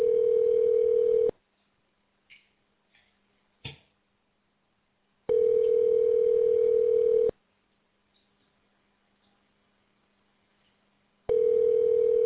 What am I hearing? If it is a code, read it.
Ringback tone (North America), 440 + 480 Hz.